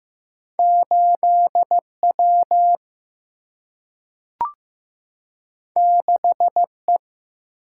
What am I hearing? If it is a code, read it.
Morse code, heard as a keyed tone, about 15 wpm, 700 Hz. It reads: 8W 6E
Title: Single-Letter Single-Number 15wpm